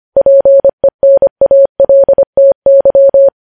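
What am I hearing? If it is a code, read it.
Morse code, heard as a keyed tone, about 25 wpm, 550 Hz. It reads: PENALTY
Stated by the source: News headlines sent in Morse at 25 wpm.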